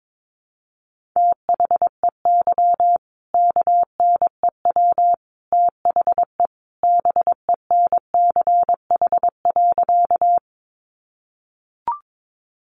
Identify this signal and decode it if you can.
Morse code, heard as a keyed tone, about 22 wpm, 700 Hz. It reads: THEY KNEW THE BENCH.
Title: Sentences from Top 1400 Words 22wpm